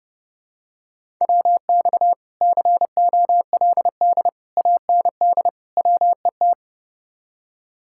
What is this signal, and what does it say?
Morse code, heard as a keyed tone, about 30 wpm, 700 Hz. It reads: WX COLD AND WET